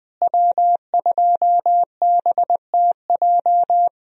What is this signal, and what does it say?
Morse code, heard as a keyed tone, about 20 wpm, 700 Hz. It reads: W2BTJ
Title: USA Call Signs - 5 Characters - Repeated 2x - 20wpm